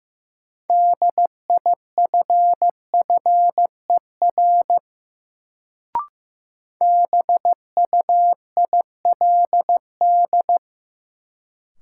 Morse code, heard as a keyed tone, about 16 wpm, 700 Hz. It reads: DIFFER BUILD
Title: Sentences from Top 1000 Words - Review Part 1 - New Word Drill - 15wpm